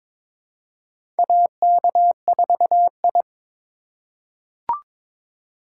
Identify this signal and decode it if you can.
Morse code, heard as a keyed tone, about 22 wpm, 700 Hz. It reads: AK4I
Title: USA Call Signs - 4 Characters - 2x Character Spacing - 22wpm